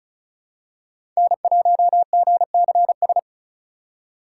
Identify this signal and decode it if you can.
Morse code, heard as a keyed tone, about 36 wpm, 700 Hz. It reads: N1GCS
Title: USA Call Signs - 3x Character Spacing - 35wpm